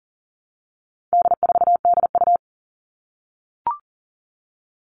Morse code, heard as a keyed tone, about 40 wpm, 700 Hz. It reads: D4DU
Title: All Cut Numbers - 4-Digit 40wpm